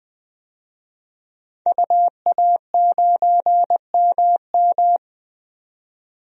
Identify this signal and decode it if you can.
Morse code, heard as a keyed tone, about 20 wpm, 700 Hz. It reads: UA9MM
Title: DXCC-100 Call Signs - 5 Characters - 20wpm